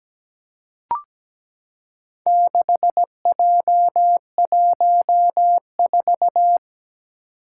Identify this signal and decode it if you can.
Morse code, heard as a keyed tone, about 17 wpm, 700 Hz. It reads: 6J14
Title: Four Challenging Letter-Number Combinations 17wpm